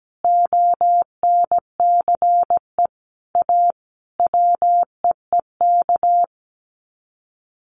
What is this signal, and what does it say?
Morse code, heard as a keyed tone, about 17 wpm, 700 Hz. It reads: ONCE A WEEK
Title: Sets of 5 Words Encore2 17wpm